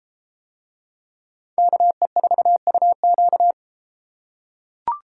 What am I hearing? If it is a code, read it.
Morse code, heard as a keyed tone, about 31 wpm, 700 Hz. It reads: KE4UQ